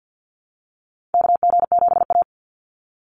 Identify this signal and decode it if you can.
Morse code, heard as a keyed tone, about 49 wpm, 700 Hz. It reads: KG7A
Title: Call Signs - 50wpm